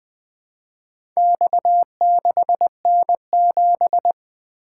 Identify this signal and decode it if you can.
Morse code, heard as a keyed tone, about 20 wpm, 700 Hz. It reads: X6N7
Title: Four Letter-Number Combinations - 20wpm